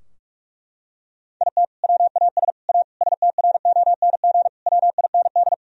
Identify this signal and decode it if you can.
Morse code, heard as a keyed tone, about 42 wpm, 700 Hz. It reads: IT WAS A STRONG WIND
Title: Sentences from Top 1000 Words - Review Part 2 - 45wpm